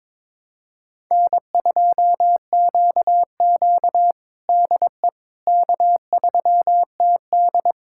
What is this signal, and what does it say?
Morse code, heard as a keyed tone, about 22 wpm, 700 Hz. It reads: N2QQ DE K3TD
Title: USA Callsign DE Callsign - 4 Characters - 22wpm